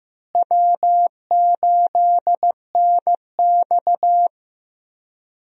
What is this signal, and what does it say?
Morse code, heard as a keyed tone, about 15 wpm, 700 Hz. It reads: W8NX
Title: USA Call Signs - 4 Characters - Repeated 3x - 15wpm